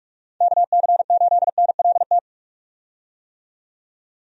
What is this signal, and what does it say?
Morse code, heard as a keyed tone, about 46 wpm, 700 Hz. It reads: KC8NLT
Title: USA Call Signs - 6 Characters - 45wpm